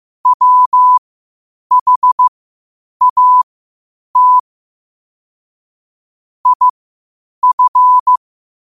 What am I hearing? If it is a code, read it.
Morse code, heard as a keyed tone, about 15 wpm, spaced as for 8 wpm, 1000 Hz. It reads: WHAT IF
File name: qotdfiles / 8wpm